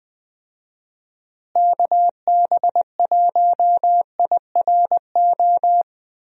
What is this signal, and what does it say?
Morse code, heard as a keyed tone, about 20 wpm, 700 Hz. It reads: KB1IRO
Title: USA Call Signs - 6 Characters - Repeated 3x - 20wpm